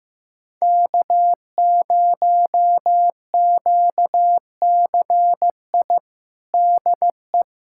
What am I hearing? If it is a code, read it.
Morse code, heard as a keyed tone, about 15 wpm, 700 Hz. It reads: K0QCI DE
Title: USA Callsign DE Callsign - 5 Characters - 15wpm